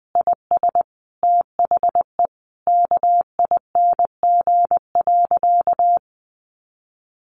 Morse code, heard as a keyed tone, about 19 wpm, 700 Hz. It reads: IS THE KING.
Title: Sentences from Top 1200 Words 20wpm